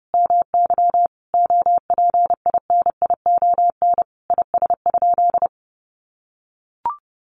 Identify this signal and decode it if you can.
Morse code, heard as a keyed tone, about 30 wpm, 700 Hz. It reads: MY OPINION IS?
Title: Sentences from Top 1200 Words 30wpm